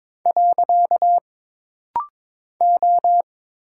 Morse code, heard as a keyed tone, about 22 wpm, 700 Hz. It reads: . O
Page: This rapid-fire timing is geared to push your ICR (Instant Character Recognition) skills.